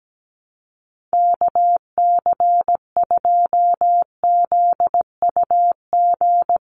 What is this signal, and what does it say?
Morse code, heard as a keyed tone, about 17 wpm, 700 Hz. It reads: KC2ZUG
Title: USA Call Signs 17wpm